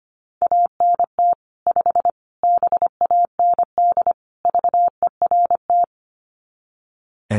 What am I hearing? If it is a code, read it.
Morse code, heard as a keyed tone, about 25 wpm, 700 Hz. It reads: ANT 5 BAND VERT
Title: Common QSO Phrases 25wpm